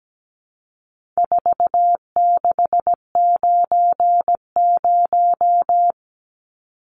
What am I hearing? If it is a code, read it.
Morse code, heard as a keyed tone, about 17 wpm, 700 Hz. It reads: 4690
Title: Numbers - 4-Digit 17wpm